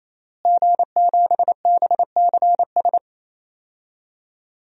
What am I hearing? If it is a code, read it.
Morse code, heard as a keyed tone, about 28 wpm, 700 Hz. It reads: G7BCS
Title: DXCC-100 Call Signs - 5 Characters - Repeated 3x - 28wpm